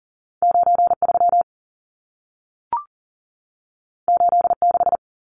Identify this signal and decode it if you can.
Morse code, heard as a keyed tone, about 40 wpm, 700 Hz. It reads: 93 86
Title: Numbers - 2-Digit 40wpm